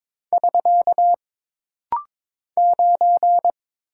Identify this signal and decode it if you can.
Morse code, heard as a keyed tone, about 22 wpm, 700 Hz. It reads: <SK> 9
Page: This rapid-fire timing is geared to push your ICR (Instant Character Recognition) skills.